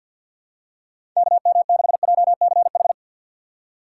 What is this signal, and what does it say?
Morse code, heard as a keyed tone, about 53 wpm, 700 Hz. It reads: KM6JCH
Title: USA Call Signs - 6 Characters - 4x Character Spacing - 50wpm